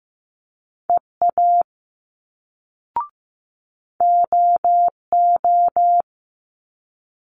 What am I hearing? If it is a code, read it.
Morse code, heard as a keyed tone, about 15 wpm, 700 Hz. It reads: EA OO
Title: Multi-letter Phonemes Quick Fire 15wpm